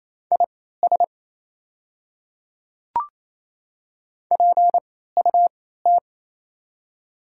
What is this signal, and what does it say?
Morse code, heard as a keyed tone, about 28 wpm, spaced as for 16 wpm, 700 Hz. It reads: IS PUT
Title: 3x Character Spacing - Top 300 Words - 28wpm